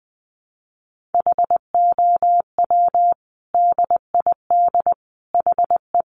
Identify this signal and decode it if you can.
Morse code, heard as a keyed tone, about 20 wpm, 700 Hz. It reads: HOW DID HE
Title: Sentences from Top 1500 Words 20wpm